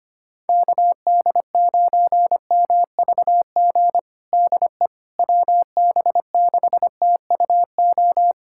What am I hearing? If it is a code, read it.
Morse code, heard as a keyed tone, about 25 wpm, 700 Hz. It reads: KD9MVG DE WB6TUO
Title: USA Callsign DE Callsign 25wpm